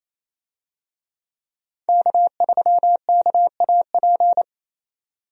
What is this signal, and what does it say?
Morse code, heard as a keyed tone, about 28 wpm, 700 Hz. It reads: K3KAP
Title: USA Call Signs - 5 Characters - 28wpm